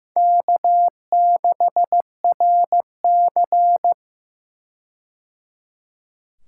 Morse code, heard as a keyed tone, about 15 wpm, 700 Hz. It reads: K6RC
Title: K1USN SST QSO Practice - 15wpm